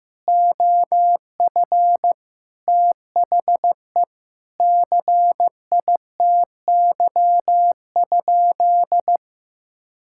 Morse code, heard as a keyed tone, about 16 wpm, 700 Hz. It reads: OF THE CITY?
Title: Sentences from Top 1000 Words - Review Part 5 - 15wpm